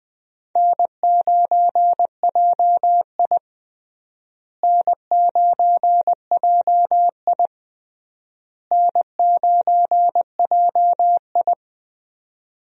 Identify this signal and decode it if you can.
Morse code, heard as a keyed tone, about 20 wpm, 690 Hz. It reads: N9JI N9JI N9JI
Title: USA Call Signs - 4 Characters - Repeated 3x - 20wpm